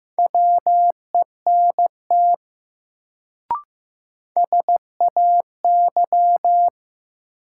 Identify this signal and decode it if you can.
Morse code, heard as a keyed tone, about 15 wpm, 700 Hz. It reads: WENT SAY
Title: Sentences from Top 200 Words - New Word Drill - 15wpm